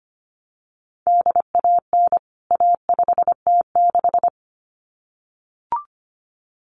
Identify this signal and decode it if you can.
Morse code, heard as a keyed tone, about 25 wpm, 700 Hz. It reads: DAN A5T6
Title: CWT Exchange 25wpm